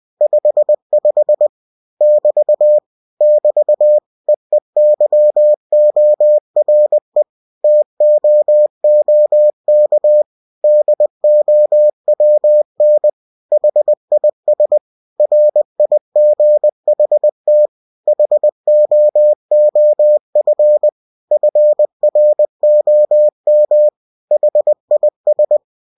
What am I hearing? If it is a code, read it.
Morse code, heard as a keyed tone, about 20 wpm, 600 Hz. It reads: 55 <BT> <BT> EEYORE TOOK DOWN HIS RIGHT HOOF FROM HIS